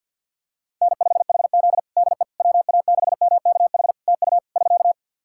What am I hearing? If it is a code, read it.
Morse code, heard as a keyed tone, about 50 wpm, 700 Hz. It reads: N5HZ DE WA6MKH TU <SK>
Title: Practice QSO - Exchange 4 - Ending - 50wpm